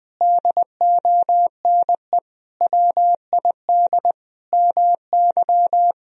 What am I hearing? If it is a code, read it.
Morse code, heard as a keyed tone, about 21 wpm, 700 Hz. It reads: DONE WID MY
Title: Sentences from Top 500 Words and Abbreviations 20wpm